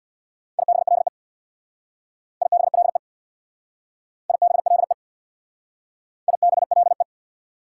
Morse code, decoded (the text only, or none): IBLE IBLE IBLE IBLE